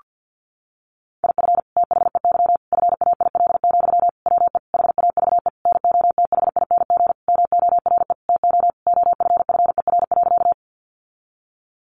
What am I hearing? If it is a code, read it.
Morse code, heard as a keyed tone, about 50 wpm, 700 Hz. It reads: IF THEY FAIL, WE HAVE NOTHING MORE TO OFFER.
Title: Sentences from Top 1100 Words 50wpm